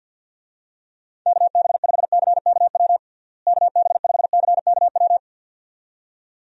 Morse code, heard as a keyed tone, about 52 wpm, 700 Hz. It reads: KB5CKW KB5CKW
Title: USA Call Signs - 6 Characters - Repeated 2x - 50wpm